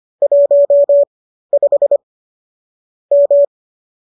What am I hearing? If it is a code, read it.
Morse code, heard as a keyed tone, about 25 wpm, spaced as for 13 wpm, 590 Hz. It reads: 15 M